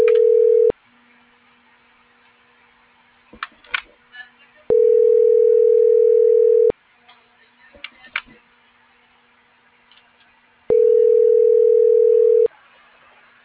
Ringback tone (North America), 440 + 480 Hz.